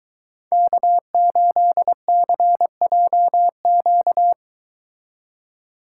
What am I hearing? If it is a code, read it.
Morse code, heard as a keyed tone, about 22 wpm, 700 Hz. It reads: K8CJQ